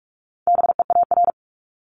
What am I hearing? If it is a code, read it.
Morse code, heard as a keyed tone, about 46 wpm, 700 Hz. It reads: BEAR
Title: Multi-letter Phonemes Repeated 3x 45wpm